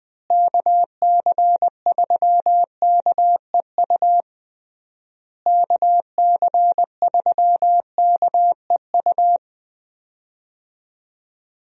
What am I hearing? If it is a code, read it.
Morse code, heard as a keyed tone, about 20 wpm, 700 Hz. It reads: KC3KEU KC3KEU
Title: USA Call Signs - 6 Characters - Repeated 3x - 20wpm